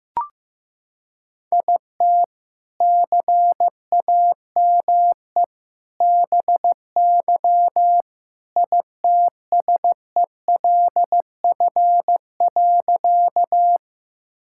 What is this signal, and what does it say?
Morse code, heard as a keyed tone, about 14 wpm, 700 Hz. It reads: IT CAME BY ITSELF.
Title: Sentences from Top 1000 Words - Review Part 5 - 15wpm